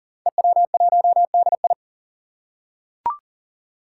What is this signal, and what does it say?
Morse code, heard as a keyed tone, about 40 wpm, 700 Hz. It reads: EW1DI
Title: Five Challenging Letter-Number Combinations 40wpm